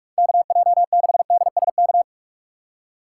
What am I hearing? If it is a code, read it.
Morse code, heard as a keyed tone, about 46 wpm, 700 Hz. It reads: KJ6DSK
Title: USA Call Signs - 6 Characters - 2x Character Spacing - 45wpm